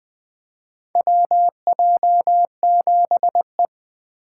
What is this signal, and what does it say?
Morse code, heard as a keyed tone, about 20 wpm, 700 Hz. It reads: WJ7E
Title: USA Call Signs - 4 Characters - 2x Character Spacing - 20wpm